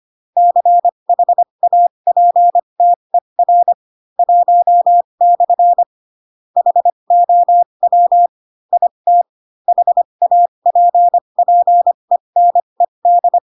Morse code, heard as a keyed tone, about 25 wpm, spaced as for 22 wpm, 700 Hz. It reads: CHAPTER 1/ HOW IT HAPPENED